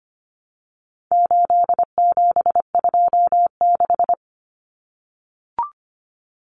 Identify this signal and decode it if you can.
Morse code, heard as a keyed tone, about 25 wpm, 700 Hz. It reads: 8726
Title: Common Cut Numbers - 4-Digit 25wpm